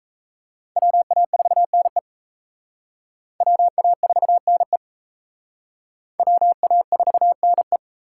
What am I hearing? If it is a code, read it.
Morse code, heard as a keyed tone, about 34 wpm, 700 Hz. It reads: WA4NE WA4NE WA4NE
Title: Speed Racing - USA Call Signs 28wpm